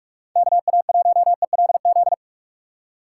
Morse code, heard as a keyed tone, about 43 wpm, 700 Hz. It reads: KA1ELZ
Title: USA Call Signs - 6 Characters - 2x Character Spacing - 45wpm